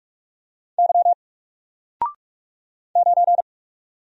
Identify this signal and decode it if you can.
Morse code, heard as a keyed tone, about 44 wpm, 700 Hz. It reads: Y 9
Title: Single Letter-Number 45wpm